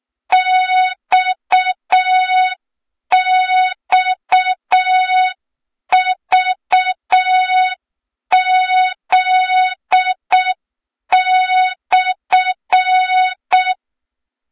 Morse code, heard as a keyed tone, about 6 wpm, 3700 Hz. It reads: XXVZ/
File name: Morze-HA2MN / hang / 23 GYAKORLAT